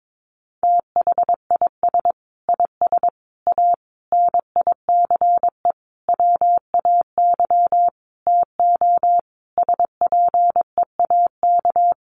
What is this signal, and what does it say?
Morse code, heard as a keyed tone, about 22 wpm, 700 Hz. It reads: THIS IS A NICE WAY TO SPEAK
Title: Sentences from Top 1100 Words 22wpm